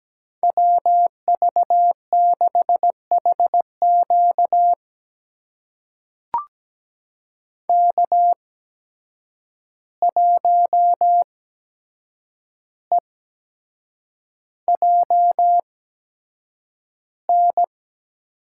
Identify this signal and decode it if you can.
Morse code, heard as a keyed tone, about 17 wpm, spaced as for 7 wpm, 700 Hz. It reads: WV6HQ K1EJN
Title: USA Call Signs - 5 Characters - 8x Character Spacing - 17wpm